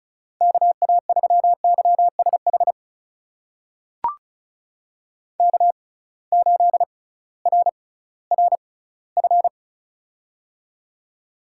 Morse code, heard as a keyed tone, about 36 wpm, spaced as for 19 wpm, 700 Hz. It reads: KA3YSH K8RRF
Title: USA Call Signs - 6x Character Spacing - 35wpm